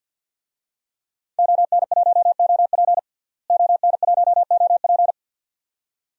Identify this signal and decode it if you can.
Morse code, heard as a keyed tone, about 49 wpm, 700 Hz. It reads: ON1OP ON1OP
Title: DXCC-100 Call Signs - 5 Characters - Repeated 2x - 50wpm